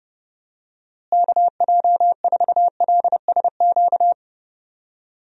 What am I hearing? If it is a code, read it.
Morse code, heard as a keyed tone, about 32 wpm, 700 Hz. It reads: KJ4LSQ
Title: USA Call Signs - 6 Characters - Repeated 2x - 30wpm